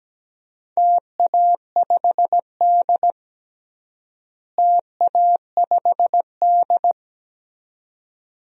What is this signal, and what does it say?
Morse code, heard as a keyed tone, about 17 wpm, 700 Hz. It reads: TA5D TA5D
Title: DXCC-100 Call Signs - 4 Characters - Repeated 2x - 17wpm